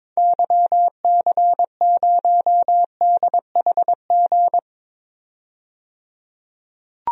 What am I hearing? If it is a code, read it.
Morse code, heard as a keyed tone, about 22 wpm, 700 Hz. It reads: YC0DHG
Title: DXCC-100 Call Signs - 6 Characters - 22wpm